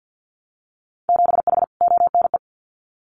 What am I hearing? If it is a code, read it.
Morse code, heard as a keyed tone, about 50 wpm, 700 Hz. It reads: 7H ONE
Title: Winter Field Day Exchanges Encore 50wpm